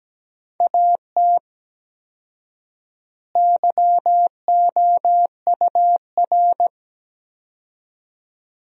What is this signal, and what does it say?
Morse code, heard as a keyed tone, about 17 wpm, 700 Hz. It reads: AT YOUR
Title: 4x Word-Spacing - Sentences from Top 200 Words - 17wpm